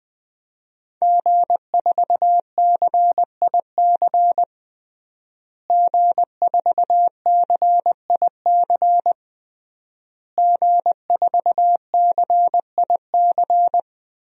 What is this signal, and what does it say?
Morse code, heard as a keyed tone, about 20 wpm, 700 Hz. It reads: G4CIC G4CIC G4CIC
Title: DXCC-100 Call Signs - 5 Characters - Repeated 3x - 20wpm